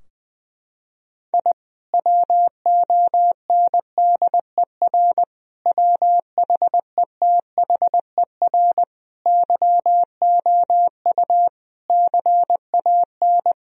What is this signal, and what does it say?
Morse code, heard as a keyed tone, about 20 wpm, 700 Hz. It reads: I WONDER WHETHER YOU CAN
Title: Sentences from Top 1000 Words - Review Part 3 - 20wpm